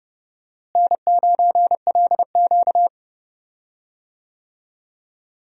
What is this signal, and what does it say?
Morse code, heard as a keyed tone, about 30 wpm, 700 Hz. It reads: N9LQ
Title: USA Call Signs - 4 Characters - Repeated 3x - 30wpm